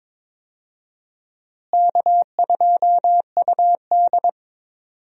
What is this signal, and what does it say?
Morse code, heard as a keyed tone, about 22 wpm, 700 Hz. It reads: K2UD
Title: USA Call Signs - 4 Characters - Repeated 2x - 22wpm